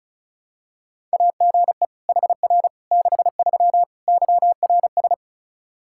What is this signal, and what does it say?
Morse code, heard as a keyed tone, about 36 wpm, 700 Hz. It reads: AGE HR 63 YRS
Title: Practice QSO - Exchange 3 - More Info - 35wpm